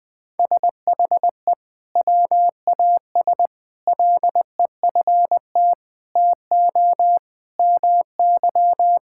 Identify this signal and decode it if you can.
Morse code, heard as a keyed tone, about 21 wpm, 700 Hz. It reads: SHE WAS LEFT TO MY
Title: Sentences from Top 1600 Words 20wpm